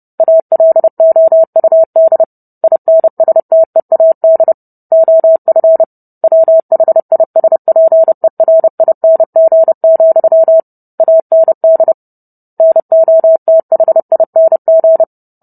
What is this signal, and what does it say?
Morse code, heard as a keyed tone, about 30 wpm, 650 Hz. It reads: ALOUD INSTEAD OF WHISPERING, AND NOTHING